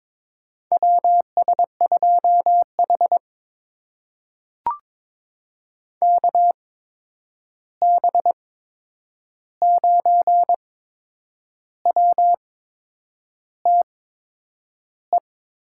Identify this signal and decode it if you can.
Morse code, heard as a keyed tone, about 22 wpm, spaced as for 8 wpm, 700 Hz. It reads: WS2H KB9WTE
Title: USA Call Signs - 8x Character Spacing - 22wpm